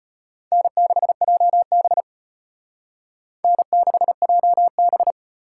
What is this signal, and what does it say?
Morse code, heard as a keyed tone, about 38 wpm, 700 Hz. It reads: N6JB N6JB